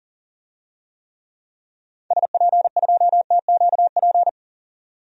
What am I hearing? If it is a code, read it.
Morse code, heard as a keyed tone, about 38 wpm, 700 Hz. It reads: SP2TQP
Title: DXCC-100 Call Signs - 6 Characters - 40wpm